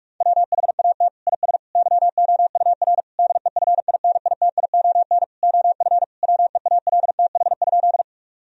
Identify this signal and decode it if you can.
Morse code, heard as a keyed tone, about 46 wpm, 700 Hz. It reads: WHAT IS YOUR DEFINITION OF WEALTH?
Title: Sentences from Top 1000 Words - Review Part 4 - 45wpm